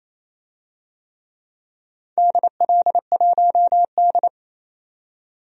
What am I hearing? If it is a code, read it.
Morse code, heard as a keyed tone, about 28 wpm, 700 Hz. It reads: DL1D
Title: DXCC-100 Call Signs - 4 Characters - 28wpm